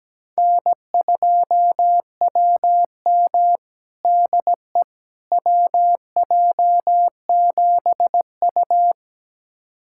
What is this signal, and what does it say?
Morse code, heard as a keyed tone, about 17 wpm, 700 Hz. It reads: N2WM DE WJ7U
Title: USA Callsign DE Callsign - 4 Characters - 17wpm